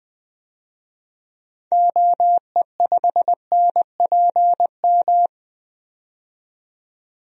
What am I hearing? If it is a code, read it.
Morse code, heard as a keyed tone, about 20 wpm, 700 Hz. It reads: OE5NPM
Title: DXCC-100 Call Signs - 6 Characters - 20wpm